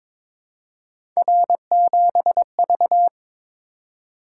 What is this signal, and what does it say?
Morse code, heard as a keyed tone, about 22 wpm, 700 Hz. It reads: R7V
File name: practice / 22wpm / Three Letter-Number Combinations - 22wpm